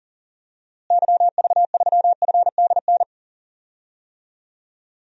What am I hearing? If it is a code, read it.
Morse code, heard as a keyed tone, about 40 wpm, 700 Hz. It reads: YV3FDN
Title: DXCC-100 Call Signs - 6 Characters - Repeated 3x - 40wpm